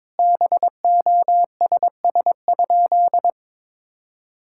Morse code, heard as a keyed tone, about 21 wpm, 700 Hz. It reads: BOSS?
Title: Sentences from Top 1000 Words - Review Part 6 - 22wpm